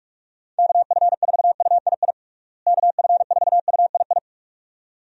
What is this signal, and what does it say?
Morse code, heard as a keyed tone, about 46 wpm, 700 Hz. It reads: KF4UII KF4UII
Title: USA Call Signs - 6 Characters - Repeated 2x - 45wpm